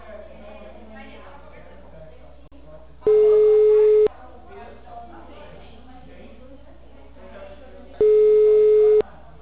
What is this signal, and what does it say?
Ringback tone (Europe, 425 Hz), with speech.